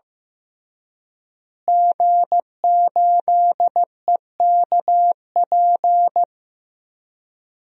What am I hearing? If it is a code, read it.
Morse code, heard as a keyed tone, about 15 wpm, 700 Hz. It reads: G8EKP
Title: DXCC-100 Call Signs - 5 Characters - Repeated 3x - 15wpm